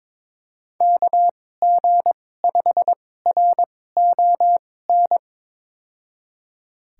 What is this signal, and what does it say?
Morse code, heard as a keyed tone, about 22 wpm, spaced as for 16 wpm, 700 Hz. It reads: KG5RON
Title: USA Call Signs - 6 Characters - 2x Character Spacing - 22wpm